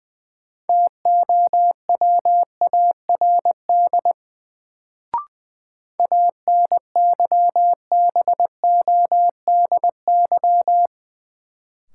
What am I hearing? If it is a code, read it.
Morse code, heard as a keyed tone, about 20 wpm, 700 Hz. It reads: TOWARD ANYBODY
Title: Sets of 5 Words - New Word Drill - 20wpm